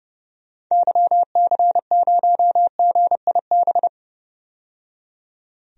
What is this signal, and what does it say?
Morse code, heard as a keyed tone, about 30 wpm, 700 Hz. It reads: YC0GIB
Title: DXCC-100 Call Signs - 6 Characters - 30wpm